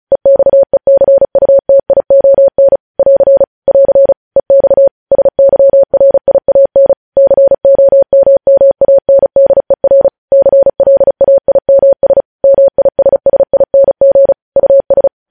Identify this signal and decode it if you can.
Morse code, heard as a keyed tone, about 36 wpm, 550 Hz. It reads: EXECUTION <AR> <AR> EX SYRIAN COMMANDER CLAIMS MISSING US
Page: news headlines in morse code at 35 wpm